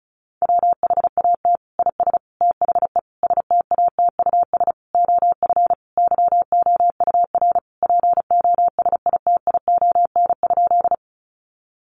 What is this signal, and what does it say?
Morse code, heard as a keyed tone, about 34 wpm, 700 Hz. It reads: WHAT IS THE STATUS OF YOUR POSITION?
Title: Sentences from Top 1400 Words 35wpm